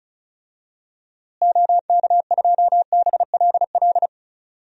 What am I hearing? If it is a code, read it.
Morse code, heard as a keyed tone, about 36 wpm, 700 Hz. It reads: OK2BLL
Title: DXCC-100 Call Signs - 6 Characters - Repeated 3x - 35wpm